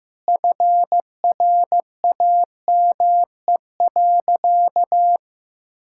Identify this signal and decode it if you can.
Morse code, heard as a keyed tone, about 15 wpm, 700 Hz. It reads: FRAME.
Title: Sentences from Top 1000 Words - Review Part 6 - 15wpm